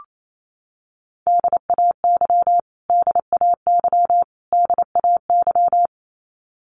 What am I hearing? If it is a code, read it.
Morse code, heard as a keyed tone, about 28 wpm, 700 Hz. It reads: DAY DAY DAY
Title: Top 200 Words Repeated 28wpm